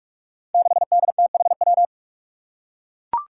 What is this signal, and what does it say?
Morse code, heard as a keyed tone, about 42 wpm, 700 Hz. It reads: 6DTHW
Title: Five Challenging Letter-Number Combinations 45wpm